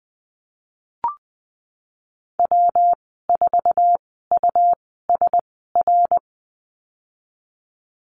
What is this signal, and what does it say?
Morse code, heard as a keyed tone, about 20 wpm, spaced as for 14 wpm, 700 Hz. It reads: W4USR
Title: 147 Lesson 24g - Letter y - Call Signs 20wpm